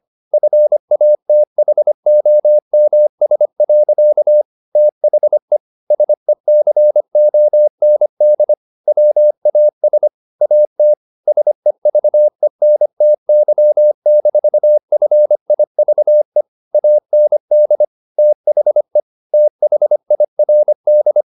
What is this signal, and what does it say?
Morse code, heard as a keyed tone, about 25 wpm, 600 Hz. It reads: FATHOMS. THE SECOND WAS AT SEVENTY-FIVE AND THE THIRD
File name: the old man and the sea part 1 25wpm